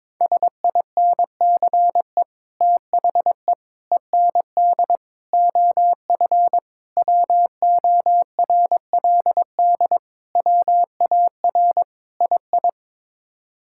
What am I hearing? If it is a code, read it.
Morse code, heard as a keyed tone, about 22 wpm, 700 Hz. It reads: SINCE THE END OF WORLD WAR II
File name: practice / 22wpm / Sets of 7 Words - 22wpm